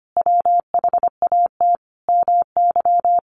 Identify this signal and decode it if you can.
Morse code, heard as a keyed tone, about 25 wpm, 700 Hz. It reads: WHAT MY
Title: Sentences from Top 1200 Words 25wpm